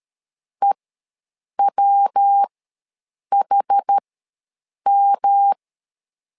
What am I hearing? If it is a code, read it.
Morse code, heard as a keyed tone, about 13 wpm, spaced as for 7 wpm, 800 Hz. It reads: EWHM
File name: LEZIONI / 3 MWRO / 3 MW ws 12wpm 1816